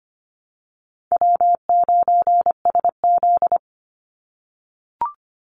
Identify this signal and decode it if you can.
Morse code, heard as a keyed tone, about 25 wpm, 700 Hz. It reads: W9SZ
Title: Easy Call Signs - 25wpm